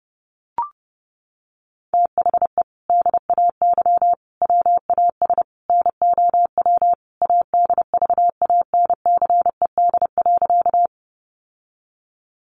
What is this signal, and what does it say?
Morse code, heard as a keyed tone, about 30 wpm, 700 Hz. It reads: THE DAY WAS NOW ADVANCED.
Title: Sentences from Top 1200 Words 30wpm